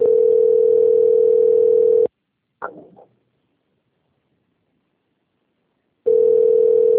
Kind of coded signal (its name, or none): telephone call progress tone